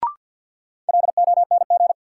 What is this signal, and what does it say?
Morse code, heard as a keyed tone, about 53 wpm, 700 Hz. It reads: LONG
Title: Sets of 2 Words - New Word Drill - 50wpm